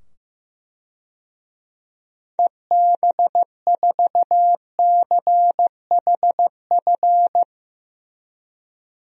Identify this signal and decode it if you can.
Morse code, heard as a keyed tone, about 15 wpm, 700 Hz. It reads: EB4CHF
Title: DXCC-100 Call Signs - 6 Characters - 15wpm